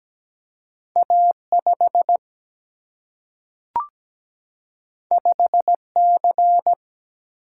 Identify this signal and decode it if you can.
Morse code, heard as a keyed tone, about 17 wpm, 700 Hz. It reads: A5 5C
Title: Single-Letter Single-Number 17wpm